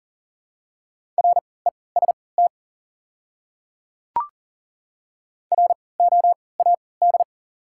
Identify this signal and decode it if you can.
Morse code, heard as a keyed tone, about 40 wpm, spaced as for 23 wpm, 700 Hz. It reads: REST ROAD
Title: 3x Character Spacing - Top 500 Words - 40wpm